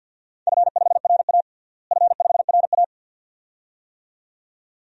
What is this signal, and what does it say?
Morse code, heard as a keyed tone, about 46 wpm, 700 Hz. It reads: F5RA F5RA
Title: DXCC-100 Call Signs - 4 Characters - Repeated 2x - 50wpm